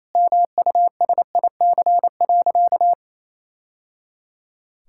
Morse code, heard as a keyed tone, about 28 wpm, 700 Hz. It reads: MUSIC.
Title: Sentences from Top 1000 Words - Review Part 6 - 28wpm